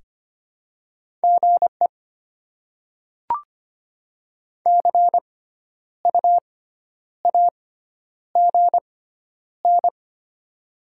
Morse code, heard as a keyed tone, about 25 wpm, spaced as for 10 wpm, 700 Hz. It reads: GE CU AGN
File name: Top 20 QSO Elements - 6x Character Spacing - 25wpm